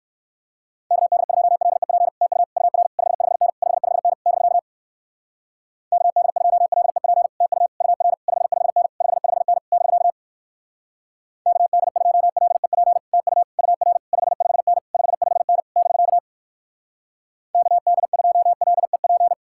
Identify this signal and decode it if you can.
Morse code, heard as a keyed tone, about 62 wpm, 700 Hz. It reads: KD2LEP TU UR 55N 55N <BK> KD2LEP TU UR 55N 55N <BK> KD2LEP TU UR 55N 55N <BK> KD2LEP